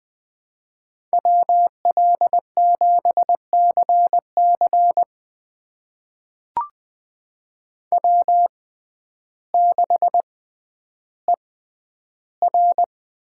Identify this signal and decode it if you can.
Morse code, heard as a keyed tone, about 20 wpm, spaced as for 11 wpm, 700 Hz. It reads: WL7CC W6ER
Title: USA Call Signs - 5 Characters - 6x Character Spacing - 20wpm